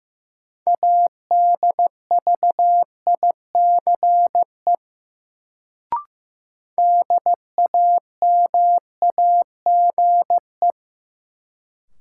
Morse code, heard as a keyed tone, about 15 wpm, 700 Hz. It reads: ADVICE DAMAGE
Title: Sentences from Top 1100 Words - New Word Drill Alt - 15wpm